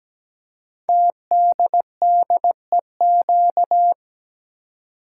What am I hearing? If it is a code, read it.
Morse code, heard as a keyed tone, about 17 wpm, 700 Hz. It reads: TDDEQ
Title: Code Groups - 5 Letters - 17wpm